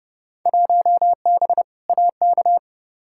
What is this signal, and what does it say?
Morse code, heard as a keyed tone, about 31 wpm, 700 Hz. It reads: 1B AK
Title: Field Day Exchanges Encore - 30wpm